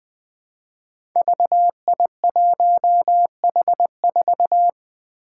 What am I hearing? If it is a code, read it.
Morse code, heard as a keyed tone, about 20 wpm, 700 Hz. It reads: VI1H4
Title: Five Challenging Letter-Number Combinations 20wpm